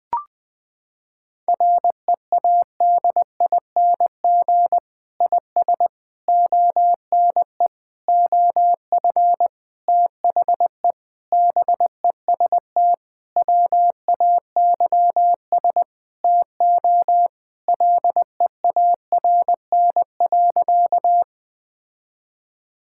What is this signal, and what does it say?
Morse code, heard as a keyed tone, about 20 wpm, 700 Hz. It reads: READING IS ONE OF THE BEST WAYS TO LEARN.
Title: Sentences with 6-Word Phrases - 20wpm